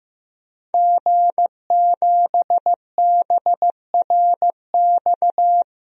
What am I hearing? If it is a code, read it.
Morse code, heard as a keyed tone, about 15 wpm, 700 Hz. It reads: G7BRX
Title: DXCC-100 Call Signs - 5 Characters - Repeated 2x - 15wpm